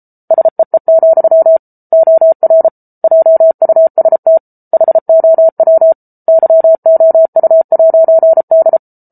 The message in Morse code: SEE, OR JUST HOW YOU'D

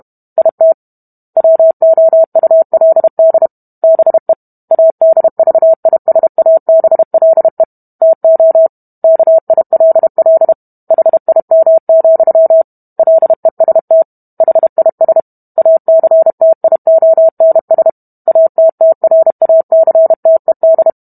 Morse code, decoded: IT WOULD BE ADVISABLE TO KILL HIM, LEST HIS ACTIONS ATTRACTED

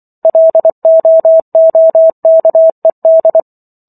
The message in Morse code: LOOKED

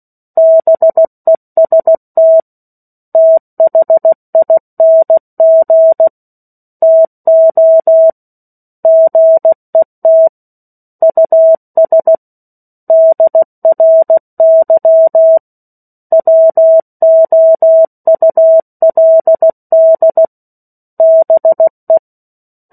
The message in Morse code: BEST THING TO GET US DRY WOULD BE

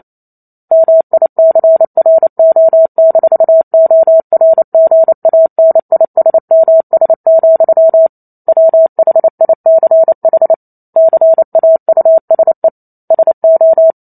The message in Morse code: MICRO-ORGANISMS, WHICH CAUSE SO

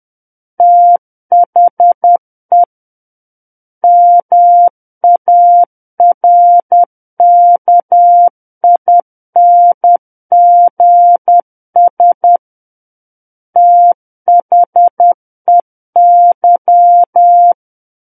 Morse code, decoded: THE MARKINGS THEY

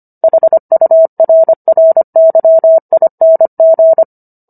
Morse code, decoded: HURRYING

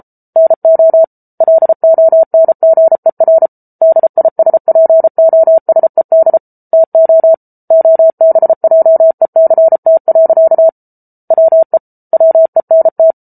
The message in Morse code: NO LONGER DISPOSED TO OBJECT. WE WENT

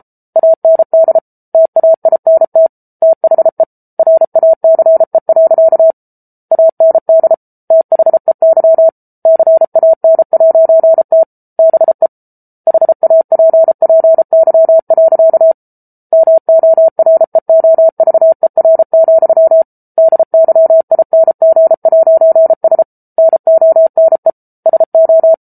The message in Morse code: AND TAINT THE RACE. AND THEY CAN'T BE HAPPY. MOREOVER, DYING'S NONE SO

